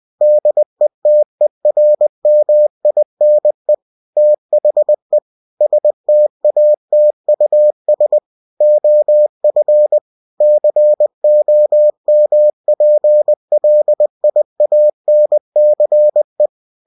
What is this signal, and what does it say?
Morse code, heard as a keyed tone, about 20 wpm, 600 Hz. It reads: DETERMINE THE STATUS OF COMPLIANCE
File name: cw-books / winnie-the-pooh / winnie-20wpm-page-0121